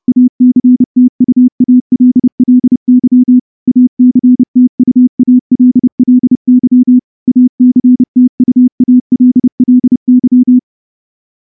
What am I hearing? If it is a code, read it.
Morse code, heard as a keyed tone, about 28 wpm, 250 Hz. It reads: ACTUALLY ACTUALLY ACTUALLY
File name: morse-training-30WPM